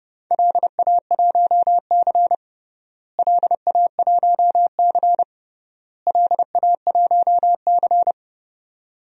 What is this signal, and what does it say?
Morse code, heard as a keyed tone, about 30 wpm, 700 Hz. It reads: LA1C LA1C LA1C
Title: DXCC-100 Call Signs - 4 Characters - Repeated 3x - 30wpm